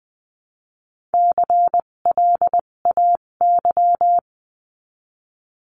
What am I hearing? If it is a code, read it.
Morse code, heard as a keyed tone, about 20 wpm, spaced as for 17 wpm, 700 Hz. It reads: CLAY
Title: 257 Lesson 42b - Words 14-20wpm